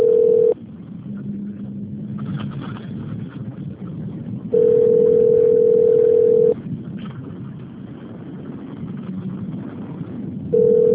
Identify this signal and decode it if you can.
Ringback tone (North America), 440 + 480 Hz.
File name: rg-1600-unknown- - - .51047